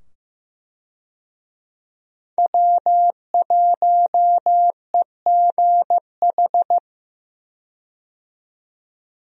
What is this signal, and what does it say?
Morse code, heard as a keyed tone, about 15 wpm, 700 Hz. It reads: W1EGH
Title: USA Call Signs - 5 Characters - 15wpm